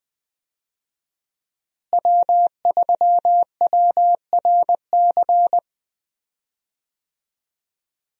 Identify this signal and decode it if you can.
Morse code, heard as a keyed tone, about 20 wpm, 700 Hz. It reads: W3WRC
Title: USA Call Signs - 5 Characters - Repeated 2x - 20wpm